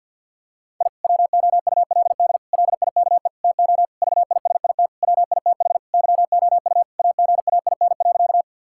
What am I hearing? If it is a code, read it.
Morse code, heard as a keyed tone, about 50 wpm, 700 Hz. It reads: I WOULD LIKE TO VISIT WITH YOU AGAIN.
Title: Sentences from Top 1000 Words - Review Part 6 - 50wpm